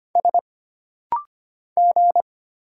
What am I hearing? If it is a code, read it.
Morse code, heard as a keyed tone, about 25 wpm, 700 Hz. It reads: S G
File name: Single Character (Letter:Number:Punctuation:Prosign) - Rapid-Fire - 25wpm